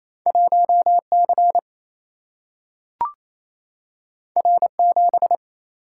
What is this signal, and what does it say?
Morse code, heard as a keyed tone, about 28 wpm, 700 Hz. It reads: 1 C R 7
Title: Single-Letter Single-Number 28wpm